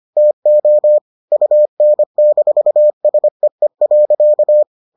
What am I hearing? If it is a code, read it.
Morse code, heard as a keyed tone, about 25 wpm, 600 Hz. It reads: TO UN-SEE.